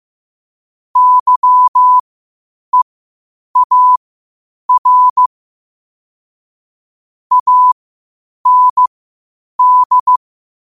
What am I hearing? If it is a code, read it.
Morse code, heard as a keyed tone, about 15 wpm, spaced as for 8 wpm, 1000 Hz. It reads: YEAR AND
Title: 8 WPM morse code quotes for Fri, 01 Mar 2024